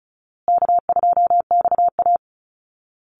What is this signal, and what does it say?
Morse code, heard as a keyed tone, about 34 wpm, 700 Hz. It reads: K2XA
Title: Easy Call Signs Repeated 3x - 35wpm